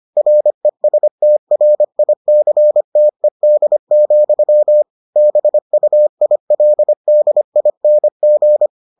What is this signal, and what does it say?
Morse code, heard as a keyed tone, about 25 wpm, 600 Hz. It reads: RESTRICTED, BUILDING